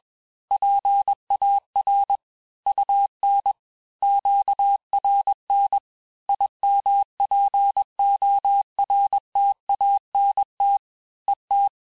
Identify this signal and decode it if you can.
Morse code, heard as a keyed tone, about 21 wpm, 800 Hz. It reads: PAR UN QRN IMPORTANT ET
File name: teste-FIN-de-QSO-vitesse-20mpm